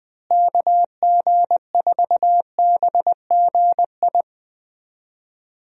Morse code, heard as a keyed tone, about 21 wpm, 700 Hz. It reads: KG4BGI
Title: USA Call Signs - 6 Characters - Repeated 2x - 20wpm